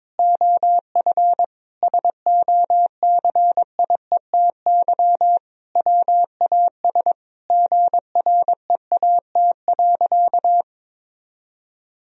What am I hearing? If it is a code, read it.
Morse code, heard as a keyed tone, about 22 wpm, 700 Hz. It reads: OF SOCIETY WAS GREAT.
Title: Sentences from Top 1000 Words - Review Part 4 - 22wpm